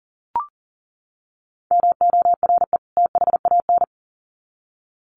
Morse code, heard as a keyed tone, about 38 wpm, 700 Hz. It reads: MORE THAN
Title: Sets of 2 Words 40wpm